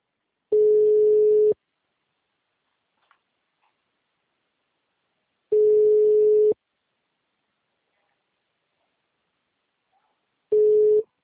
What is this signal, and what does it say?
Ringback tone (Europe, 425 Hz).